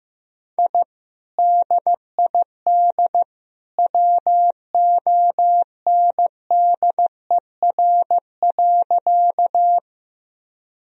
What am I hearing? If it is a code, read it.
Morse code, heard as a keyed tone, about 16 wpm, 700 Hz. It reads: I DID WONDER.
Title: Sentences from Top 1000 Words - Review Part 3 - 15wpm